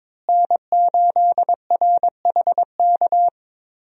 Morse code, heard as a keyed tone, about 22 wpm, 700 Hz. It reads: N8RHK
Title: USA Call Signs - 5 Characters - Repeated 3x - 22wpm